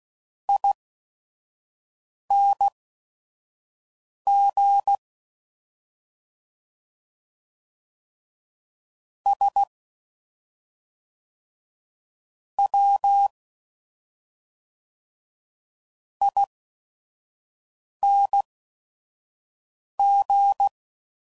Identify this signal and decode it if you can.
Morse code, heard as a keyed tone, about 16 wpm, spaced as for 5 wpm, 800 Hz. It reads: ING S W ING